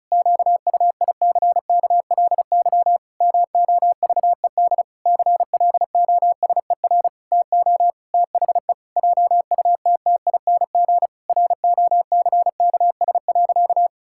Morse code, heard as a keyed tone, about 36 wpm, 700 Hz. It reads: QUICKLY MOVED CLOSER TO THE JUTTING ROCKS.
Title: Pangram Challenge v2 - 35wpm